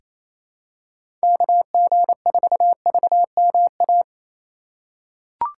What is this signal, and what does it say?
Morse code, heard as a keyed tone, about 27 wpm, 700 Hz. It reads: KG4VMA
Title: USA Call Signs - 6 Characters - 2x Character Spacing - 28wpm